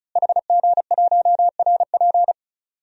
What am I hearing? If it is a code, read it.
Morse code, heard as a keyed tone, about 36 wpm, 700 Hz. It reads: HG1RP
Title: DXCC-100 Call Signs - 5 Characters - Repeated 3x - 35wpm